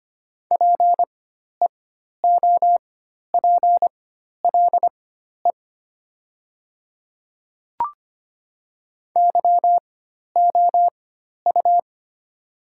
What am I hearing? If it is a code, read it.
Morse code, heard as a keyed tone, about 25 wpm, spaced as for 12 wpm, 700 Hz. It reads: PEOPLE YOU
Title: 4x Character Spacing - Top 100 Words - 25wpm